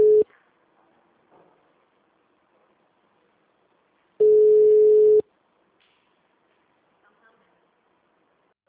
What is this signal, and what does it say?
Ringback tone (Europe, 425 Hz).